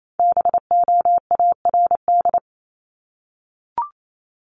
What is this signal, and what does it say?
Morse code, heard as a keyed tone, about 28 wpm, 700 Hz. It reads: BOARD
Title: Top 501-750 Words 28wpm